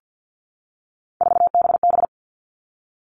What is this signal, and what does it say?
Morse code, heard as a keyed tone, about 50 wpm, 700 Hz. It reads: 4BB
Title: All Cut Numbers - 3-Digit 50wpm